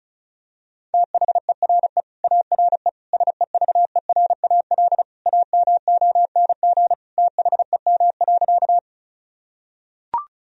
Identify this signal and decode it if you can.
Morse code, heard as a keyed tone, about 36 wpm, 700 Hz. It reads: THERE ARE SEVERAL AMONG THEM.
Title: Sentences from Top 1000 Words - Review Part 2 - 35wpm